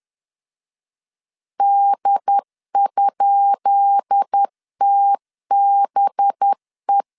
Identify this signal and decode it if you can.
Morse code, heard as a keyed tone, about 10 wpm, 800 Hz. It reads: D?TBE
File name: LEZIONI / 10 SIGNS / 10B INT 10wpm 2844